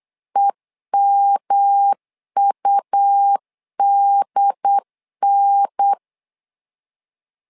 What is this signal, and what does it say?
Morse code, heard as a keyed tone, about 8 wpm, 800 Hz. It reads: EMUDN
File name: LEZIONI / 3 MWRO / 3 M 8wpm 2300